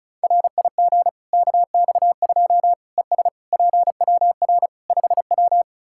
Morse code, heard as a keyed tone, about 34 wpm, 700 Hz. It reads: RIG KX2 ES PWR 5W
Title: K1USN SST QSO Practice 35wpm